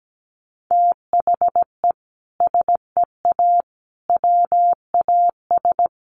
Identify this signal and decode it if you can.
Morse code, heard as a keyed tone, about 17 wpm, 700 Hz. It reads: THE SEA WAS
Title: Sentences from Top 1200 Words 17wpm